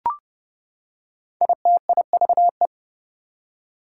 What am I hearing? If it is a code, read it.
Morse code, heard as a keyed tone, about 30 wpm, 700 Hz. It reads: ITIVE
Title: Prefixes and Suffixes 30wpm